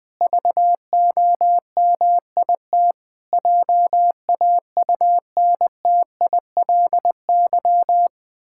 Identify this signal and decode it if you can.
Morse code, heard as a keyed tone, about 21 wpm, 700 Hz. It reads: VOMIT JAUNTILY
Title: Pangram Challenge v2 - 20wpm